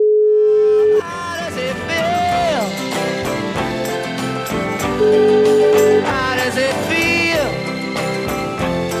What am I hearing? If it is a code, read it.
Ringback tone (Europe, 425 Hz), with speech.